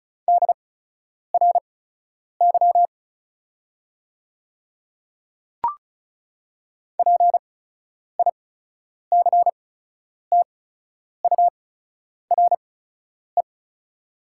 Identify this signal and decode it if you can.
Morse code, heard as a keyed tone, about 36 wpm, spaced as for 10 wpm, 700 Hz. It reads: DRY PICTURE
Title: 8x Character Spacing - Top 500 Words - 35wpm